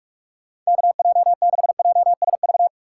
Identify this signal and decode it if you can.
Morse code, heard as a keyed tone, about 45 wpm, 700 Hz. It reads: KJ6JSV